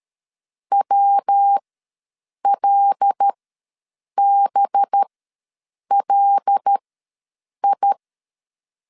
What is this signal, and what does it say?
Morse code, heard as a keyed tone, about 13 wpm, spaced as for 7 wpm, 800 Hz. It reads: WLBLI
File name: LEZIONI / 5 LBVC / 5 LB ws 12wpm 3034